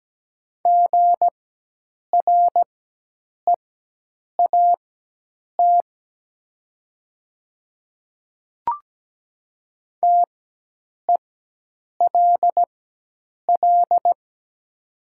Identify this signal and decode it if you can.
Morse code, heard as a keyed tone, about 17 wpm, spaced as for 8 wpm, 700 Hz. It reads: GREAT TELL
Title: 4x Character Spacing - Top 200 Words - 17wpm